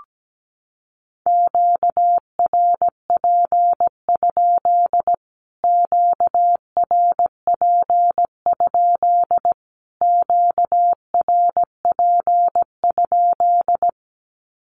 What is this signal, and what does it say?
Morse code, heard as a keyed tone, about 17 wpm, 700 Hz. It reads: QRP? QRP? QRP?
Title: Q-Codes Repeated 3x 17wpm